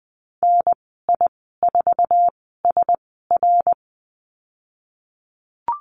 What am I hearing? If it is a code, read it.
Morse code, heard as a keyed tone, about 20 wpm, spaced as for 14 wpm, 700 Hz. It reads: NI4SR
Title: 082 Lesson 14g - Number 2 - Call Signs 20wpm